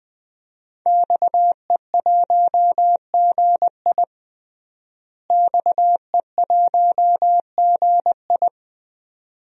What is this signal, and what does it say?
Morse code, heard as a keyed tone, about 20 wpm, 700 Hz. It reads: XE1GI XE1GI
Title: DXCC-100 Call Signs - 5 Characters - Repeated 2x - 20wpm